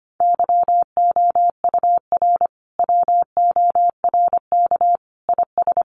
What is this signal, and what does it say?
Morse code, heard as a keyed tone, about 25 wpm, 700 Hz. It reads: YOUR WORK IS
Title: Sentences from Top 1100 Words 25wpm